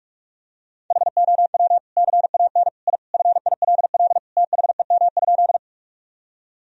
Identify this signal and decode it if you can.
Morse code, heard as a keyed tone, about 45 wpm, 700 Hz. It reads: HOW CAN I FILL THEM?
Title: Sentences from Top 1000 Words - Review Part 3 - 45wpm